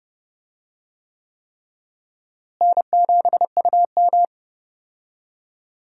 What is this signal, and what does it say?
Morse code, heard as a keyed tone, about 30 wpm, 700 Hz. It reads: N7UM
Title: USA Call Signs - 4 Characters - Repeated 3x - 30wpm